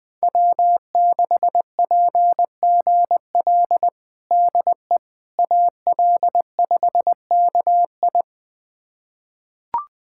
Morse code, heard as a keyed tone, about 20 wpm, 700 Hz. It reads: W6PGL DE AL5KI
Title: USA Callsign DE Callsign - 5 Characters - 20wpm